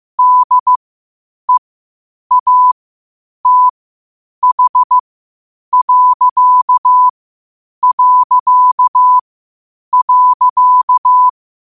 Morse code, heard as a keyed tone, about 15 wpm, spaced as for 8 wpm, 1000 Hz. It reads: DEATH...
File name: qotdfiles / 8wpm